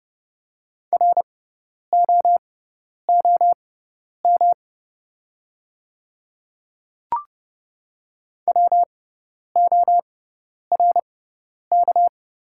Morse code, heard as a keyed tone, about 30 wpm, spaced as for 10 wpm, 700 Hz. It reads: ROOM WORK
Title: 6x Character Spacing - Top 300 Words - 30wpm